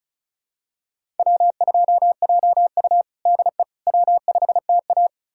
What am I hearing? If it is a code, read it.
Morse code, heard as a keyed tone, about 36 wpm, 700 Hz. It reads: W2JU DE W5TA
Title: USA Callsign DE Callsign - 4 Characters - 35wpm